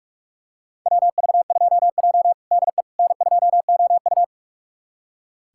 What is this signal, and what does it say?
Morse code, heard as a keyed tone, about 41 wpm, 700 Hz. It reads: WV2J DE N2OU
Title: USA Callsign DE Callsign - 4 Characters - 45wpm